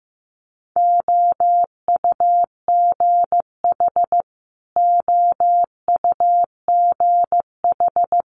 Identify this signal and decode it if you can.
Morse code, heard as a keyed tone, about 16 wpm, 700 Hz. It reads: OUGH OUGH
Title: Multi-letter Phonemes Repeated 3x 15wpm